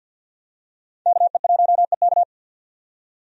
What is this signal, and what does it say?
Morse code, heard as a keyed tone, about 52 wpm, 700 Hz. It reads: KE1EK
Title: USA Call Signs - 5 Characters - 50wpm